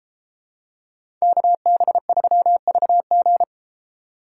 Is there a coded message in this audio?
Yes: Morse code from 1.2 to 3.4 s, heard as a keyed tone, about 32 wpm, 700 Hz. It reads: KB3VG